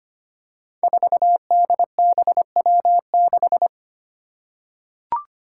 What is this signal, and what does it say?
Morse code, heard as a keyed tone, about 25 wpm, 700 Hz. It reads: 4DBW6
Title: Five Challenging Letter-Number Combinations 25wpm